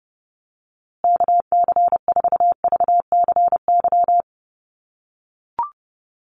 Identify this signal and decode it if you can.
Morse code, heard as a keyed tone, about 29 wpm, 700 Hz. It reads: KC4VCY
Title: USA Call Signs Repeated 2x 30wpm